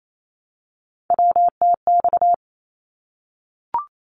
Morse code, heard as a keyed tone, about 28 wpm, 700 Hz. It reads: WTX
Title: ARRL Sections 28wpm